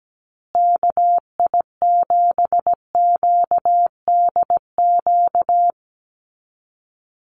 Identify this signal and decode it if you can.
Morse code, heard as a keyed tone, about 17 wpm, 700 Hz. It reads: KI7QDQ
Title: USA Call Signs Repeated 2x 17wpm